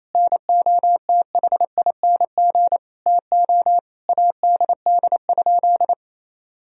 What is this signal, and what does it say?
Morse code, heard as a keyed tone, about 28 wpm, 700 Hz. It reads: NOTHING TO ADD?
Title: Sentences from Top 1000 Words - Review Part 1 - 28wpm